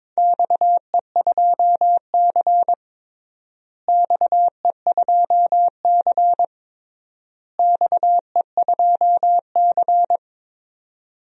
Morse code, heard as a keyed tone, about 21 wpm, 700 Hz. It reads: XE2C XE2C XE2C
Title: DXCC-100 Call Signs - 4 Characters - Repeated 3x - 22wpm